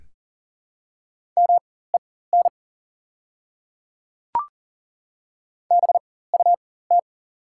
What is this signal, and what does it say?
Morse code, heard as a keyed tone, about 40 wpm, spaced as for 19 wpm, 690 Hz. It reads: MEN BUT
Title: 4x Character Spacing - Top 200 Words - 40wpm